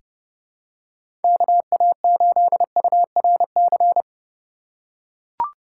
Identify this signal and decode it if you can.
Morse code, heard as a keyed tone, about 32 wpm, 700 Hz. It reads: KA8URC
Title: USA Call Signs - 6 Characters - 4x Character Spacing - 30wpm